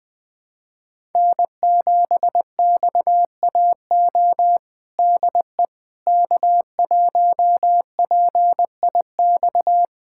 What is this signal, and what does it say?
Morse code, heard as a keyed tone, about 20 wpm, 700 Hz. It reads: N7XAO DE K1PIX
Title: USA Callsign DE Callsign - 5 Characters - 20wpm